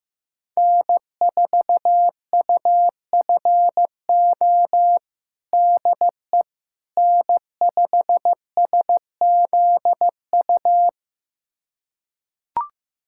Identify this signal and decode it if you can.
Morse code, heard as a keyed tone, about 15 wpm, 700 Hz. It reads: N4UFO DE N5SZU
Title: USA Callsign DE Callsign - 5 Characters - 15wpm